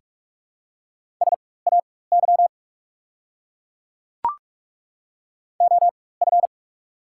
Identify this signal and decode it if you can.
Morse code, heard as a keyed tone, about 43 wpm, spaced as for 21 wpm, 690 Hz. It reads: SAY OF
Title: 4x Character Spacing - Top 200 Words - 45wpm